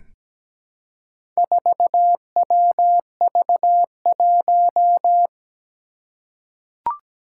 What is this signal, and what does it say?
Morse code, heard as a keyed tone, about 17 wpm, 700 Hz. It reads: 4WV1
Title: Four Challenging Letter-Number Combinations 17wpm